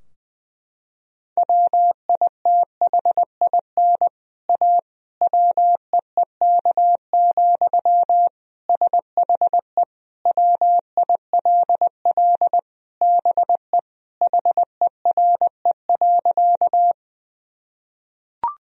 Morse code, heard as a keyed tone, about 19 wpm, 700 Hz. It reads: WITHIN A WEEK, SHE WILL BE HERE.
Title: Sentences from Top 1000 Words - Review Part 4 - 20wpm